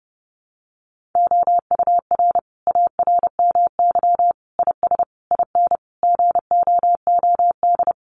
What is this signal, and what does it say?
Morse code, heard as a keyed tone, about 30 wpm, 700 Hz. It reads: OUR ARMY IS IN GOOD
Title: Sentences from Top 1500 Words 30wpm